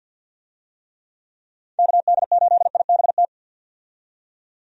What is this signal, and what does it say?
Morse code, heard as a keyed tone, about 51 wpm, 700 Hz. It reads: KD8IBT
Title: USA Call Signs - 6 Characters - 50wpm